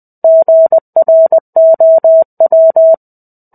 Morse code, heard as a keyed tone, about 20 wpm, 650 Hz. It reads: GROW
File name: 05-Chapter-05 20 WPM